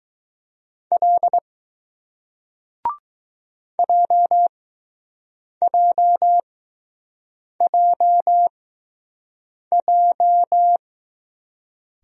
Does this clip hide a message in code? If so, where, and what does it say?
Yes: Morse code from 0.9 to 10.8 s, heard as a keyed tone, about 20 wpm, spaced as for 6 wpm, 700 Hz. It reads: L JJJJ